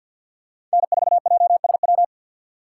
Morse code, heard as a keyed tone, about 52 wpm, 700 Hz. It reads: N4JSW
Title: USA Call Signs - 5 Characters - 8x Character Spacing - 50wpm